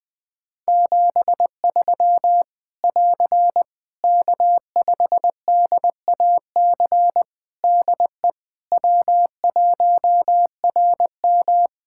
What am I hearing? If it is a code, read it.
Morse code, heard as a keyed tone, about 20 wpm, 700 Hz. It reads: 73 <AR> K5DAC DE W1RM
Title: Practice QSO - Exchange 4 - Ending - 20wpm